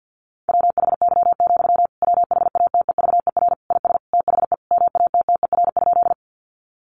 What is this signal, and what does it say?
Morse code, heard as a keyed tone, about 50 wpm, 700 Hz. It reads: WHY, WHATEVER IS THE MATTER?
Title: Sentences from Top 1100 Words 50wpm